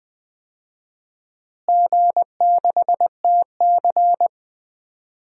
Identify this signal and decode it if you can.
Morse code, heard as a keyed tone, about 20 wpm, 700 Hz. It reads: G6TC
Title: DXCC-100 Call Signs - 4 Characters - Repeated 2x - 20wpm